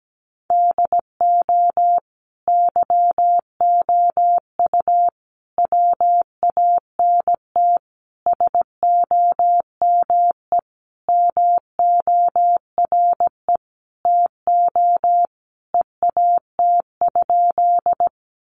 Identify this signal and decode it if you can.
Morse code, heard as a keyed tone, about 17 wpm, 700 Hz. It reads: DO YOU WANT SOME MORE TO EAT?
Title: Sentences from Top 300 Words 17wpm